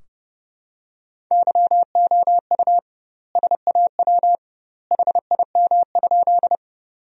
Morse code, heard as a keyed tone, about 30 wpm, 700 Hz. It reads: YOU SAW HIM?
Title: 2x Word-Spacing - Sentences from Top 300 Words - 30wpm